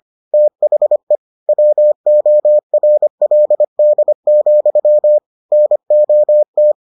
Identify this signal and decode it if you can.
Morse code, heard as a keyed tone, about 25 wpm, 600 Hz. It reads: THE WORLD, NOT